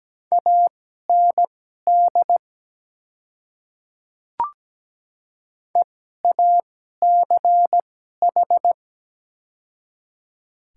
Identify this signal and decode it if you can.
Morse code, heard as a keyed tone, about 17 wpm, spaced as for 12 wpm, 700 Hz. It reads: AND EACH
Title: 2x Character Spacing - Top 100 Words - 17wpm